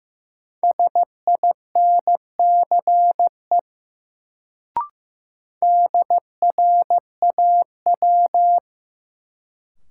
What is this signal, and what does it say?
Morse code, heard as a keyed tone, about 15 wpm, 700 Hz. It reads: SINCE DRAW
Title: Sentences from Top 300 Words - New Word Drill - 15wpm